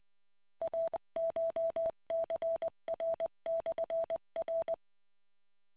Morse code, heard as a keyed tone, about 20 wpm, 660 Hz. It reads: ACHCR/R